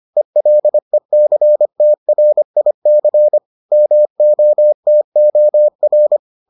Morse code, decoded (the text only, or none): ELECTRIC MOTOR